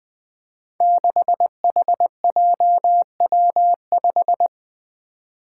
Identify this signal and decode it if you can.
Morse code, heard as a keyed tone, about 20 wpm, 700 Hz. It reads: 6HJW5
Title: Five Challenging Letter-Number Combinations 20wpm